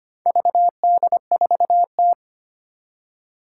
Morse code, heard as a keyed tone, about 25 wpm, 700 Hz. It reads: VD4T
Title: Four Challenging Letter-Number Combinations 25wpm